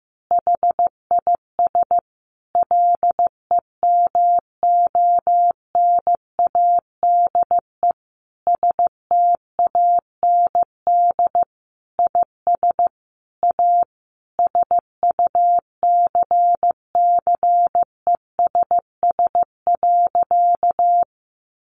Morse code, heard as a keyed tone, about 15 wpm, 700 Hz. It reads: HIS LEMONADE STAND IS A SUCCESS.
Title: Additional Easy Phrases 15wpm